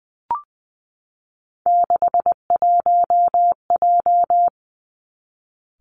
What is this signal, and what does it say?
Morse code, heard as a keyed tone, about 20 wpm, 700 Hz. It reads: 61J
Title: Three Challenging Letter-Number Combinations 20wpm